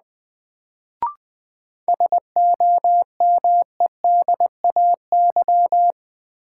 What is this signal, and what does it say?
Morse code, heard as a keyed tone, about 20 wpm, 700 Hz. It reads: SOMEDAY
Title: Sentences from Top 1600 Words - New Word Drill Alt - 20wpm